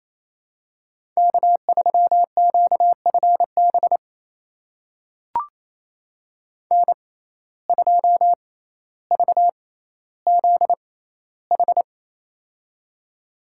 Morse code, heard as a keyed tone, about 28 wpm, spaced as for 14 wpm, 700 Hz. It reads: K3QFB N2VZH
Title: USA Call Signs - 5 Characters - 6x Character Spacing - 28wpm